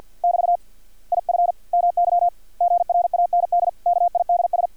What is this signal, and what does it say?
Morse code, heard as a keyed tone, about 50 wpm, 700 Hz. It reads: <BT> IF MY GRAND KIDS
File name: archived / CRSnet